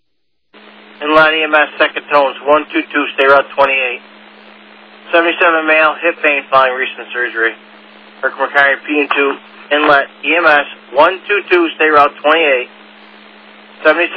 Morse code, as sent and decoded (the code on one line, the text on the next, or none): none